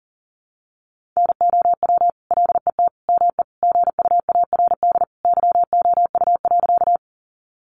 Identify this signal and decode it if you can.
Morse code, heard as a keyed tone, about 40 wpm, 700 Hz. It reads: NOW LET ME GUARD YOU.
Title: Sentences from Top 700 Words 40wpm